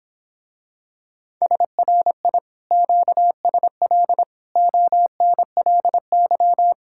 Morse code, heard as a keyed tone, about 26 wpm, 700 Hz. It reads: SRI QSL ONLY